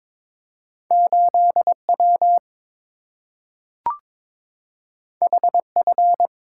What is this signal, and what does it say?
Morse code, heard as a keyed tone, about 22 wpm, 690 Hz. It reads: 8W HF
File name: practice / 22wpm / Two Letter-Number Combinations - 22wpm